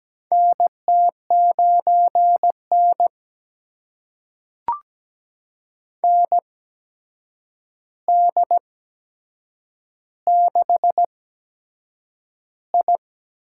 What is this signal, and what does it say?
Morse code, heard as a keyed tone, about 17 wpm, spaced as for 7 wpm, 700 Hz. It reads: NT9N ND6I
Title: USA Call Signs - 4 Characters - 8x Character Spacing - 17wpm